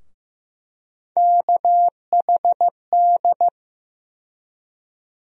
Morse code, heard as a keyed tone, about 15 wpm, 700 Hz. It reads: KHD
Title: USA Call Sign Postfixes - 3 Letters 15wpm